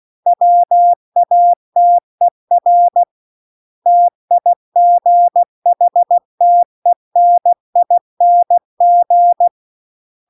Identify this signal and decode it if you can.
Morse code, heard as a keyed tone, about 16 wpm, 700 Hz. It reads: WATER TIGHTENING
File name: CW-The-Big-Twoheaded-River-2-16WPM